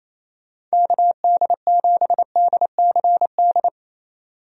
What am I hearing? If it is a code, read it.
Morse code, heard as a keyed tone, about 28 wpm, 700 Hz. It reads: KD7DCD
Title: USA Call Signs - 6 Characters - 28wpm